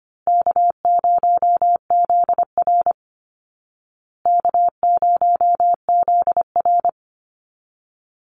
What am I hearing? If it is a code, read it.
Morse code, heard as a keyed tone, about 25 wpm, 700 Hz. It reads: K0ZR K0ZR
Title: Easy Call Signs Repeated 3x - 25wpm